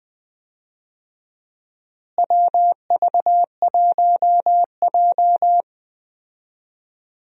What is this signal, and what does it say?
Morse code, heard as a keyed tone, about 20 wpm, 700 Hz. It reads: WV1J
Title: USA Call Signs - 4 Characters - Repeated 2x - 20wpm